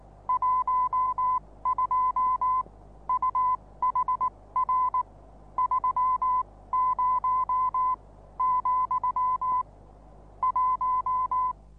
Morse code, heard as a keyed tone, about 19 wpm, spaced as for 16 wpm, 1010 Hz. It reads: 12 UHR 30 , 1
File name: repos / Morsekurs / DJ4UF Morsen26